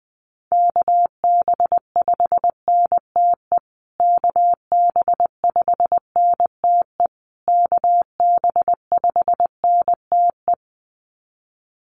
Morse code, heard as a keyed tone, about 20 wpm, 700 Hz. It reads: KB5NTE KB5NTE KB5NTE
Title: USA Call Signs Repeated 3x 20wpm